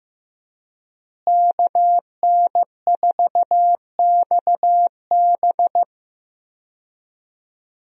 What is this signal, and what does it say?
Morse code, heard as a keyed tone, about 15 wpm, 700 Hz. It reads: KN4XB
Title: USA Call Signs - 5 Characters - Repeated 3x - 15wpm